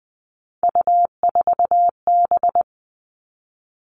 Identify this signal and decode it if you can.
Morse code, heard as a keyed tone, about 20 wpm, 700 Hz. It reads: U4B
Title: Three Challenging Letter-Number Combinations 20wpm